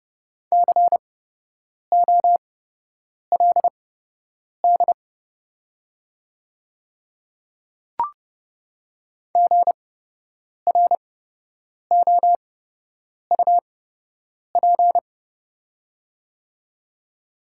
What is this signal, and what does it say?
Morse code, heard as a keyed tone, about 29 wpm, spaced as for 8 wpm, 700 Hz. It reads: COLD GROUP
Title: 8x Character Spacing - Top 500 Words - 30wpm